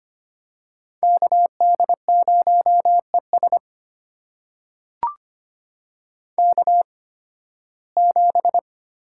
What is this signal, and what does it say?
Morse code, heard as a keyed tone, about 25 wpm, spaced as for 16 wpm, 700 Hz. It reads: KD0ES K7
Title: USA Call Signs - 8x Character Spacing - 25wpm